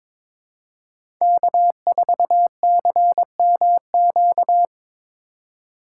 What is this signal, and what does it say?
Morse code, heard as a keyed tone, about 22 wpm, 700 Hz. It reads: K4CMQ
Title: USA Call Signs - 3x Character Spacing - 22wpm